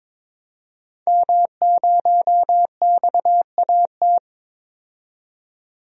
Morse code, heard as a keyed tone, about 21 wpm, 700 Hz. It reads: M0XAT
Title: DXCC-100 Call Signs - 5 Characters - 22wpm